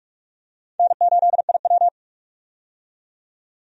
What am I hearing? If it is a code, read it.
Morse code, heard as a keyed tone, about 41 wpm, 700 Hz. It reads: N8IW
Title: USA Call Signs - 4 Characters - 45wpm